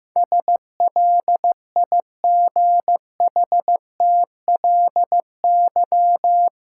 Morse code, heard as a keyed tone, about 15 wpm, 700 Hz. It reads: SLIGHTLY
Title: Sentences from Top 1300 Words - New Word Drill - 15wpm